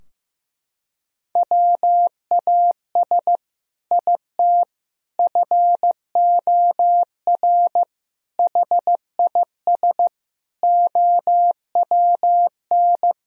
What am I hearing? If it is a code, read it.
Morse code, heard as a keyed tone, about 15 wpm, 700 Hz. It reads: WAS IT FOR HIS OWN
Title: Sentences from Top 1000 Words - Review Part 3 - 15wpm